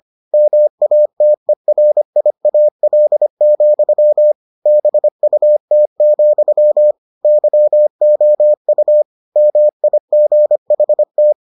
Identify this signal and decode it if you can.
Morse code, heard as a keyed tone, about 25 wpm, 600 Hz. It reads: MATERIAL, BUT, YOU MIGHT